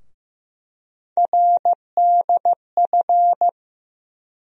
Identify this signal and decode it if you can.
Morse code, heard as a keyed tone, about 15 wpm, 700 Hz. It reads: RDF
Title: USA Call Sign Postfixes - 3 Letters 15wpm